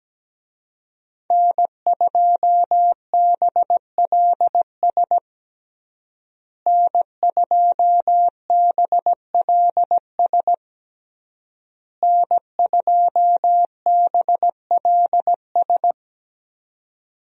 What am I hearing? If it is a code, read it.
Morse code, heard as a keyed tone, about 17 wpm, 700 Hz. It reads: N2BLS N2BLS N2BLS
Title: USA Call Signs - 5 Characters - Repeated 3x - 17wpm